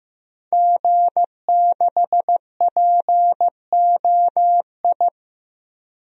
Morse code, heard as a keyed tone, about 15 wpm, 700 Hz. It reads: G6POI
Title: DXCC-100 Call Signs - 5 Characters - 15wpm